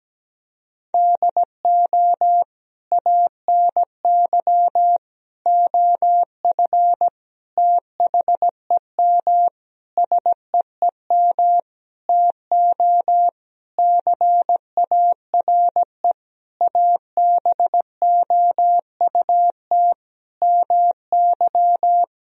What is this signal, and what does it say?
Morse code, heard as a keyed tone, about 17 wpm, 700 Hz. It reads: DO ANY OF THEM SEEM TO CARE ABOUT MY
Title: Sentences from Top 1000 Words - Review Part 3 - 17wpm